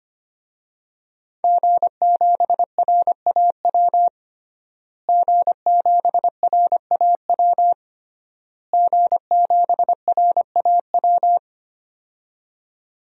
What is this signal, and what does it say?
Morse code, heard as a keyed tone, about 25 wpm, 700 Hz. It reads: G7RAW G7RAW G7RAW
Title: DXCC-100 Call Signs - 5 Characters - Repeated 3x - 25wpm